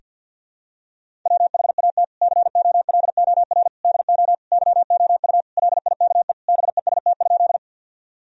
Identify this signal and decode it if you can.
Morse code, heard as a keyed tone, about 51 wpm, 700 Hz. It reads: WHAT COLOR DO YOU LIKE BEST?
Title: Sentences from Top 1000 Words - Review Part 2 - 50wpm